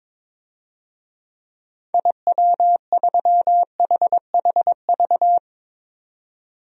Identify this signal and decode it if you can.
Morse code, heard as a keyed tone, about 22 wpm, 700 Hz. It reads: IW3HHV
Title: DXCC-100 Call Signs - 6 Characters - 22wpm